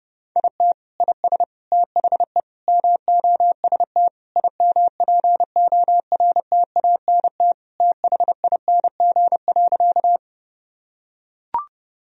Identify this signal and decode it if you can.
Morse code, heard as a keyed tone, about 30 wpm, 700 Hz. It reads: IT IS THE MOST IMPORTANT THING.
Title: Sentences from Top 1000 Words - Review Part 1 - 30wpm